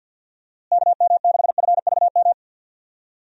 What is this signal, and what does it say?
Morse code, heard as a keyed tone, about 47 wpm, 700 Hz. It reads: KM6FVM
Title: USA Call Signs - 6 Characters - Repeated 2x - 50wpm